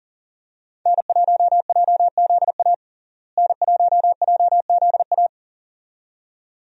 Morse code, heard as a keyed tone, about 40 wpm, 700 Hz. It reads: N1JZA N1JZA
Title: USA Call Signs - 5 Characters - Repeated 2x - 40wpm